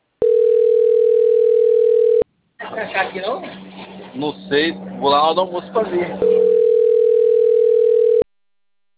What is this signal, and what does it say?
Ringback tone (North America), 440 + 480 Hz, with speech.
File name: rg-1600-unknown- - - .3486